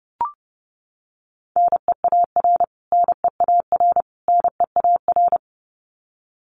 Morse code, heard as a keyed tone, about 30 wpm, 700 Hz. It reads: NEAR NEAR NEAR
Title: Top 500 Words Repeated 30wpm